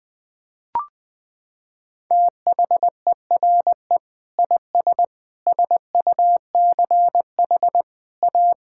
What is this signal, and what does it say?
Morse code, heard as a keyed tone, about 21 wpm, 700 Hz. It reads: THERE IS SUCH A
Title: Sets of 6 Words - 20wpm